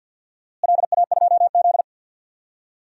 Morse code, heard as a keyed tone, about 53 wpm, 700 Hz. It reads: LA2Z
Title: DXCC-100 Call Signs - 4 Characters - Repeated 3x - 50wpm